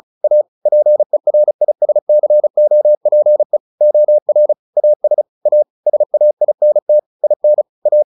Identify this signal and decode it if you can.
Morse code, heard as a keyed tone, about 34 wpm, 600 Hz. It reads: A PERISCOPE OR AS A SAINT IN A
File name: the old man and the sea 2 35wpm